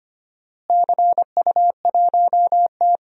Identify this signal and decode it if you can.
Morse code, heard as a keyed tone, about 25 wpm, 700 Hz. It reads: CU1T
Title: DXCC-100 Call Signs - 4 Characters - Repeated 3x - 25wpm